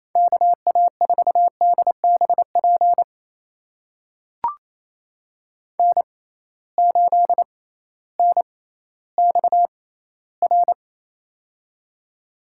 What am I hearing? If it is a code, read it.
Morse code, heard as a keyed tone, about 28 wpm, spaced as for 15 wpm, 700 Hz. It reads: KA4DBP N8NXR
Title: USA Call Signs - 6x Character Spacing - 28wpm